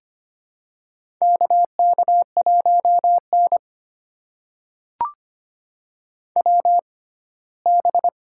Morse code, heard as a keyed tone, about 25 wpm, spaced as for 17 wpm, 700 Hz. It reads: KK1N WB
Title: USA Call Signs - 6x Character Spacing - 25wpm